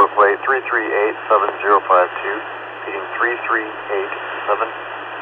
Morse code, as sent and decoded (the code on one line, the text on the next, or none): none